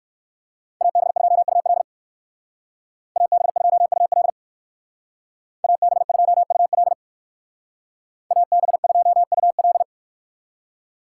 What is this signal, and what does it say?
Morse code, heard as a keyed tone, about 58 wpm, 700 Hz. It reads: AB2UL AB2UL AB2UL AB2UL